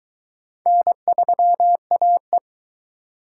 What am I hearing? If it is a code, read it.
Morse code, heard as a keyed tone, about 22 wpm, 700 Hz. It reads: N3AE